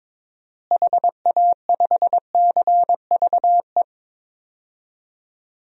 Morse code, heard as a keyed tone, about 22 wpm, 700 Hz. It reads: HA5CVE
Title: DXCC-100 Call Signs - 6 Characters - Repeated 2x - 22wpm